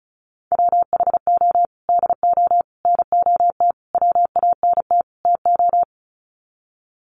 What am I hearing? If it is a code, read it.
Morse code, heard as a keyed tone, about 34 wpm, 700 Hz. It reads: WHO DO NOT WANT TO
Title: Sets of 5 Words Encore2 35wpm